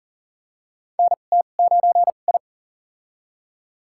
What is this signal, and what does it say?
Morse code, heard as a keyed tone, about 40 wpm, spaced as for 29 wpm, 700 Hz. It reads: NT9I
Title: USA Call Signs - 2x Character Spacing - 40wpm